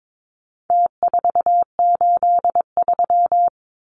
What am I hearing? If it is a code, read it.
Morse code, heard as a keyed tone, about 22 wpm, 700 Hz. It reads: T483
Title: Numbers - Contest Style - 4-Digit 22wpm